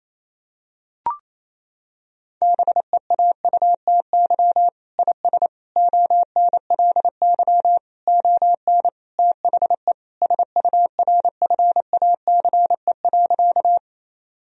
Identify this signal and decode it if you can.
Morse code, heard as a keyed tone, about 28 wpm, 700 Hz. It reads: BEAUTY IS ONLY ON THE SURFACE.
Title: Sentences from Top 1000 Words - Review Part 3 - 28wpm